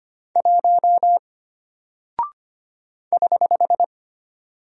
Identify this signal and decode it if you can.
Morse code, heard as a keyed tone, about 25 wpm, 700 Hz. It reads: 1 <HH>
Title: Single Character (Letter:Number:Punctuation:Prosign) Alt 25wpm